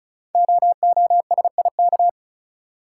Morse code, heard as a keyed tone, about 36 wpm, 690 Hz. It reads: OOSIK
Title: Code Groups - 5 Letters - 35wpm